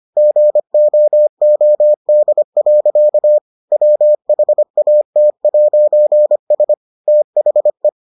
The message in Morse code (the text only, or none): GOOD. WHAT'S THE